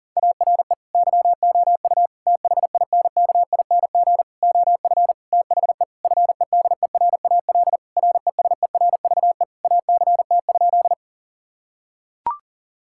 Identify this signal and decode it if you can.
Morse code, heard as a keyed tone, about 40 wpm, 700 Hz. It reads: ARE YOU THINKING OF THE FEDERAL RESERVE ACT?
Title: Sentences from Top 1000 Words - Review Part 6 - 40wpm